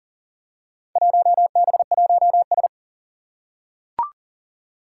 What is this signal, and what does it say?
Morse code, heard as a keyed tone, about 37 wpm, 700 Hz. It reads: 1B1S
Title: Four Challenging Letter-Number Combinations 40wpm